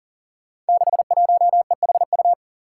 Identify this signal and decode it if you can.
Morse code, heard as a keyed tone, about 40 wpm, 700 Hz. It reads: 61EHU
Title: Five Challenging Letter-Number Combinations 40wpm